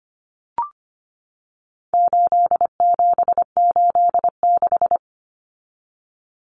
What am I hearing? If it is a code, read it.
Morse code, heard as a keyed tone, about 25 wpm, 700 Hz. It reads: 8786
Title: Numbers - 4-Digit 25wpm